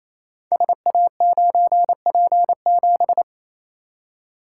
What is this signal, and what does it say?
Morse code, heard as a keyed tone, about 28 wpm, 700 Hz. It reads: SA9P7
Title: Code Groups - 5 Letter:Numbers - 28wpm